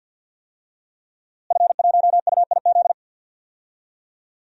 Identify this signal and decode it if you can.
Morse code, heard as a keyed tone, about 52 wpm, 700 Hz. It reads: F1UIZ
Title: DXCC-100 Call Signs - 5 Characters - Repeated 2x - 50wpm